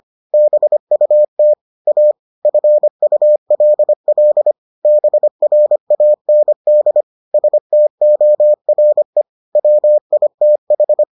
Morse code, heard as a keyed tone, about 25 wpm, 600 Hz. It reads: BUT A FULL BRAND STORE WITH